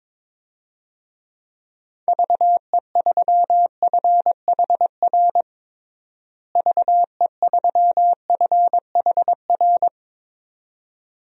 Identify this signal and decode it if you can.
Morse code, heard as a keyed tone, about 22 wpm, 700 Hz. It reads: VE3FHR VE3FHR
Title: DXCC-100 Call Signs - 6 Characters - Repeated 2x - 22wpm